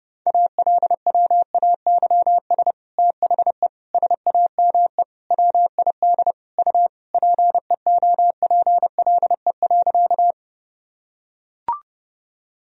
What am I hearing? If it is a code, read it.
Morse code, heard as a keyed tone, about 30 wpm, 700 Hz. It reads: ALWAYS THE SAME WID U PEOPLE.
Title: Sentences from Top 300 Words and Abbreviations 30wpm